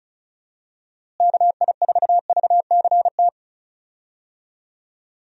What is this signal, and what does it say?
Morse code, heard as a keyed tone, about 34 wpm, 700 Hz. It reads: KI4VCT
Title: USA Call Signs - 6 Characters - Repeated 3x - 35wpm